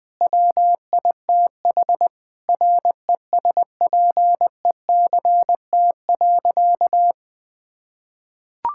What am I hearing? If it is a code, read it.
Morse code, heard as a keyed tone, about 21 wpm, 700 Hz. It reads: WITH RESPECT.
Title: Sentences with 4-Word Phrases - Spoken Beginning - 20wpm